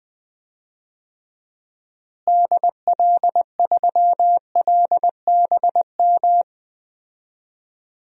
Morse code, heard as a keyed tone, about 20 wpm, 700 Hz. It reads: DL3LBM
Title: DXCC-100 Call Signs - 6 Characters - Repeated 3x - 20wpm